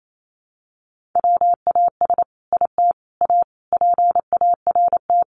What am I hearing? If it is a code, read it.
Morse code, heard as a keyed tone, about 28 wpm, 700 Hz. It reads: WAS IT A PART
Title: Sentences from Top 1500 Words 28wpm